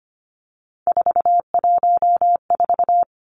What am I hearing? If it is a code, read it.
Morse code, heard as a keyed tone, about 25 wpm, 700 Hz. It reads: 414
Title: Three Challenging Letter-Number Combinations 25wpm